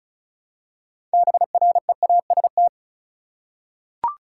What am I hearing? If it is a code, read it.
Morse code, heard as a keyed tone, about 36 wpm, 700 Hz. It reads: BREAST
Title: Six Letter Words - 35wpm